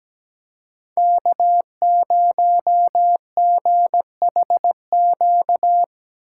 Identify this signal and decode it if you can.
Morse code, heard as a keyed tone, about 17 wpm, 700 Hz. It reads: K0GHQ
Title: USA Call Signs - 5 Characters - 2x Character Spacing - 17wpm